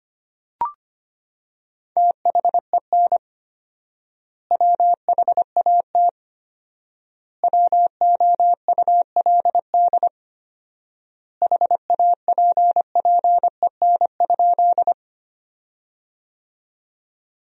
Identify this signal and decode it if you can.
Morse code, heard as a keyed tone, about 25 wpm, 700 Hz. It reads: THEN WHAT WOULD HAPPEN?
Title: 4x Word-Spacing - Sentences from Top 400 Words - 25wpm